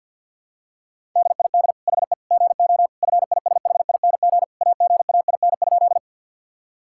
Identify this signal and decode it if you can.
Morse code, heard as a keyed tone, about 53 wpm, 700 Hz. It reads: DID HE GO FISHING AGAIN?
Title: Sentences from Top 1000 Words - Review Part 5 - 50wpm